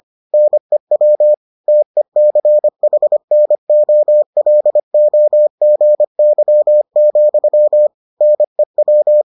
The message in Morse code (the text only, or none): NEW TECHNOLOGY, NEW